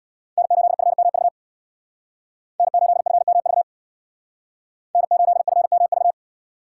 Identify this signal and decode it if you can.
Morse code, heard as a keyed tone, about 63 wpm, 700 Hz. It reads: N8FMV N8FMV N8FMV